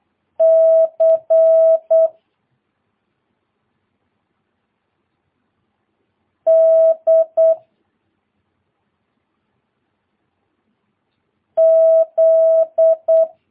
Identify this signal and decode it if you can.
Morse code, heard as a keyed tone, about 8 wpm, 650 Hz. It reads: C D Z
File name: jh8cbh / 2022cw